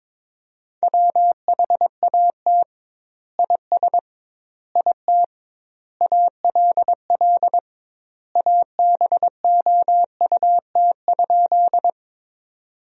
Morse code, heard as a keyed tone, about 22 wpm, 700 Hz. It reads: WHAT IS IT ALL ABOUT?
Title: 2x Word-Spacing - Sentences from Top 100 Words - 22wpm